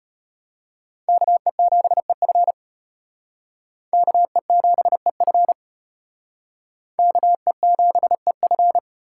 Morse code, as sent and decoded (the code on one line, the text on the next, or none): -.- . --... ..-. / -.- . --... ..-. / -.- . --... ..-.
KE7EF KE7EF KE7EF